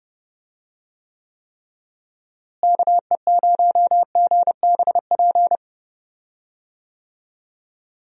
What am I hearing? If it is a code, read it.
Morse code, heard as a keyed tone, about 31 wpm, 700 Hz. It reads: KE0GBP
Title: USA Call Signs - 6 Characters - Repeated 3x - 30wpm